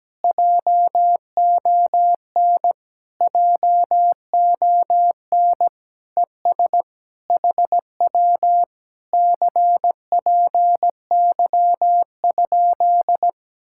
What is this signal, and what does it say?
Morse code, heard as a keyed tone, about 17 wpm, 700 Hz. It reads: JON JON ES HW CPY?
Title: Speed-Racing Common QSO Phrases - 17wpm